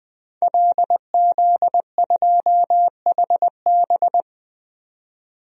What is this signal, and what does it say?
Morse code, heard as a keyed tone, about 20 wpm, 700 Hz. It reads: LZ2HB
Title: DXCC-100 Call Signs - 5 Characters - 20wpm